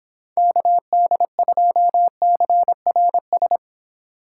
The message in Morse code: KD2CRS

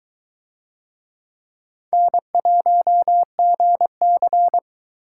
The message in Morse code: N1GC

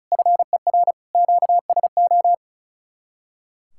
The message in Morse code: FER QSO

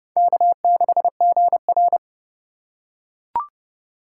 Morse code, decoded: K6GR

